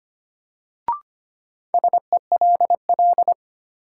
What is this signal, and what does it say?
Morse code, heard as a keyed tone, about 25 wpm, 700 Hz. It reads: SELL
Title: Sentences from Top 1100 Words - New Word Drill Alt - 25wpm